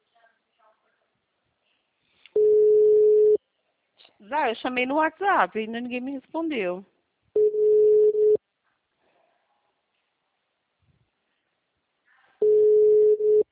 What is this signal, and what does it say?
Ringback tone (Europe, 425 Hz), with speech.